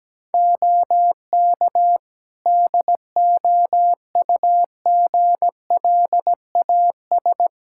Morse code, heard as a keyed tone, about 17 wpm, 700 Hz. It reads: OK DOUGLAS
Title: Practice QSO - Exchange 3 - More Info - 17wpm